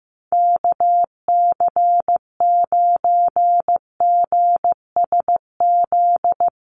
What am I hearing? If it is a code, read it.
Morse code, heard as a keyed tone, about 15 wpm, 700 Hz. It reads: KC9GSZ
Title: USA Call Signs Repeated 3x 15wpm